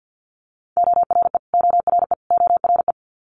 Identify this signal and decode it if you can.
Morse code, heard as a keyed tone, about 50 wpm, 700 Hz. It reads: ORE ORE ORE
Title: Multi-letter Phonemes Repeated 3x 50wpm